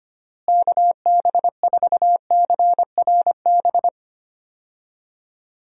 Morse code, heard as a keyed tone, about 25 wpm, 700 Hz. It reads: KB4CRB
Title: USA Call Signs - 6 Characters - Repeated 2x - 25wpm